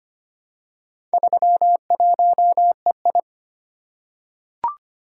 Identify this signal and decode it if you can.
Morse code, heard as a keyed tone, about 25 wpm, 700 Hz. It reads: 31EI
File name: practice / 25wpm / Four Letter-Number Combinations - 25wpm